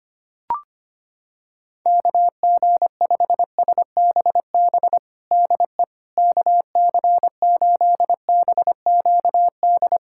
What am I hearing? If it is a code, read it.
Morse code, heard as a keyed tone, about 25 wpm, 700 Hz. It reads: KG5SBB DE KC8BQD
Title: USA Callsign DE Callsign - 6 Characters - 25wpm